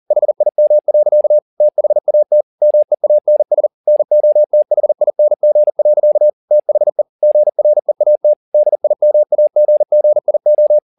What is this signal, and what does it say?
Morse code, heard as a keyed tone, about 40 wpm, 600 Hz. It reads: HIM. THAT MEANS NOTHING. THE GREAT DIMAGGIO